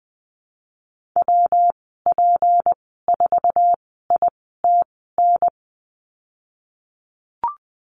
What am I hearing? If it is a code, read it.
Morse code, heard as a keyed tone, about 20 wpm, spaced as for 15 wpm, 700 Hz. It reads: WP4ITN
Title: 161 Lesson 26g - Letter g - Call Signs 20wpm